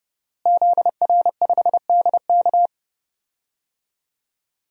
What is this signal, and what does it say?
Morse code, heard as a keyed tone, about 30 wpm, 700 Hz. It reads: ZR5DK
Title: DXCC-100 Call Signs - 5 Characters - Repeated 2x - 30wpm